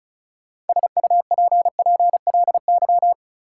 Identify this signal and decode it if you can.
Morse code, heard as a keyed tone, about 36 wpm, 700 Hz. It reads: SUPPLY
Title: Sentences from Top 1200 Words - New Word Drill Alt - 35wpm